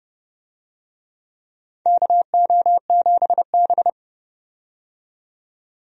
Morse code, heard as a keyed tone, about 30 wpm, 700 Hz. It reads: KO7B
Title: USA Call Signs - 4 Characters - 30wpm